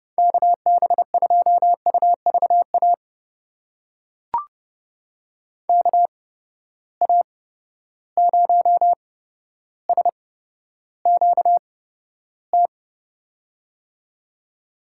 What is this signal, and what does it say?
Morse code, heard as a keyed tone, about 30 wpm, spaced as for 13 wpm, 700 Hz. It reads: KB2UVA KA0SQT
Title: USA Call Signs - 6 Characters - 8x Character Spacing - 30wpm